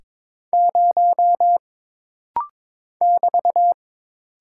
Morse code, heard as a keyed tone, about 22 wpm, 700 Hz. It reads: 0 <BT>
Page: This rapid-fire timing is geared to push your ICR (Instant Character Recognition) skills.